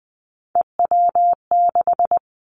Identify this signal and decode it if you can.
Morse code, heard as a keyed tone, about 20 wpm, 700 Hz. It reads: EW6
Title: Callsign Prefixes for Top 50 DXCC - 20wpm